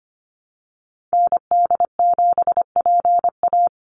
Morse code, heard as a keyed tone, about 25 wpm, 700 Hz. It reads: ND7PA
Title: NA SOTA Activator Callsigns 25wpm